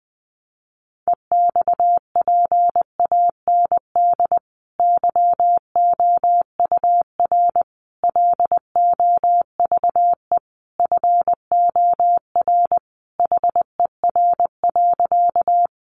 Morse code, heard as a keyed tone, about 20 wpm, 700 Hz. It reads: EXPAND YOUR LOVE FOR HER.
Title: Sentences from Top 1400 Words 20wpm